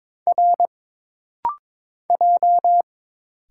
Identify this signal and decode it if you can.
Morse code, heard as a keyed tone, about 22 wpm, 700 Hz. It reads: R J
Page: This rapid-fire timing is geared to push your ICR (Instant Character Recognition) skills.